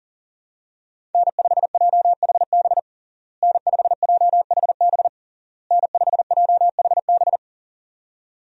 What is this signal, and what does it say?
Morse code, heard as a keyed tone, about 40 wpm, 700 Hz. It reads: N5JHB N5JHB N5JHB
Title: USA Call Signs - 5 Characters - Repeated 3x - 40wpm